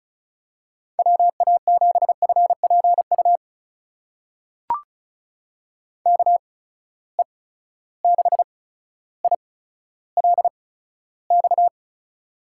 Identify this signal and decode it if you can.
Morse code, heard as a keyed tone, about 36 wpm, spaced as for 15 wpm, 700 Hz. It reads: WA7FPU KE6ILX
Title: USA Call Signs - 6 Characters - 8x Character Spacing - 35wpm